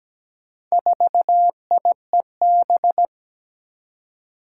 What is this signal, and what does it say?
Morse code, heard as a keyed tone, about 17 wpm, 700 Hz. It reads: 4IEB
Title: Four Challenging Letter-Number Combinations 17wpm